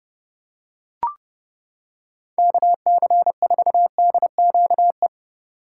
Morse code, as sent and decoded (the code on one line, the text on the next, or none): -.- -.-. ....- -.. --.- .
KC4DQE